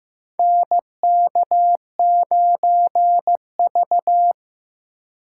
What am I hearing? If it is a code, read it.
Morse code, heard as a keyed tone, about 15 wpm, 700 Hz. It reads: NK9V
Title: USA Call Signs - 4 Characters - 6x Character Spacing - 15wpm